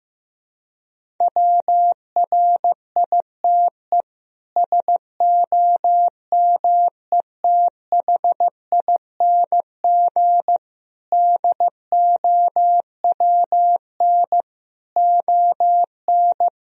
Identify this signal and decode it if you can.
Morse code, heard as a keyed tone, about 16 wpm, 700 Hz. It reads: WRITE SOMETHING DOWN ON
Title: Sentences from Top 1000 Words - Review Part 1 - 15wpm